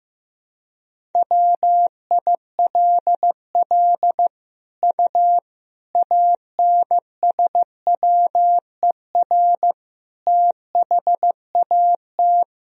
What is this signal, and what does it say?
Morse code, heard as a keyed tone, about 15 wpm, 700 Hz. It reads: WILL U ANSWER THAT
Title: Sentences from Top 400 Words and Abbreviations 15wpm